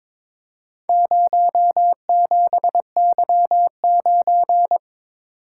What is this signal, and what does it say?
Morse code, heard as a keyed tone, about 23 wpm, 700 Hz. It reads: 07Y9
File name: practice / 22wpm / Four Letter-Number Combinations - 22wpm